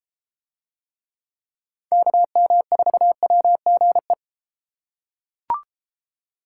Morse code, heard as a keyed tone, about 33 wpm, 700 Hz. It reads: KM4WGE